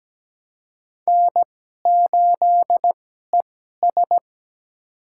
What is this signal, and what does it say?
Morse code, heard as a keyed tone, about 17 wpm, spaced as for 12 wpm, 700 Hz. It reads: N8ES
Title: USA Call Signs - 2x Character Spacing - 17wpm